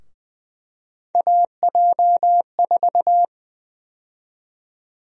Morse code, heard as a keyed tone, about 20 wpm, 700 Hz. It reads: AJ4
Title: USA Call Sign Prefixes - Repeated 2x - 20wpm